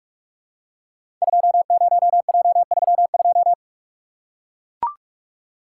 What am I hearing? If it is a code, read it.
Morse code, heard as a keyed tone, about 44 wpm, 700 Hz. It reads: 20J32
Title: Code Groups - 5 Letter:Numbers - 45wpm